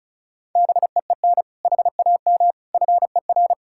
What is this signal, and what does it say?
Morse code, heard as a keyed tone, about 36 wpm, 700 Hz. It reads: BEEN HAM FER
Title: Practice QSO - Exchange 3 - More Info - 35wpm